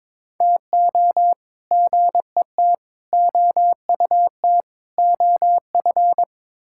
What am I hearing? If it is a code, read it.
Morse code, heard as a keyed tone, about 22 wpm, 700 Hz. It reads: TO GET OUT OF
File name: practice / 22wpm / Build A Phrase - 5-Word Phrases - 22wpm